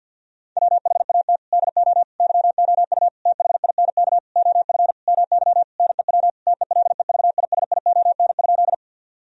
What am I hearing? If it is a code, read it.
Morse code, heard as a keyed tone, about 50 wpm, 700 Hz. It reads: WHAT DO YOU THINK OF MY NEW TELEVISION?
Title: Sentences from Top 1000 Words - Review Part 4 - 50wpm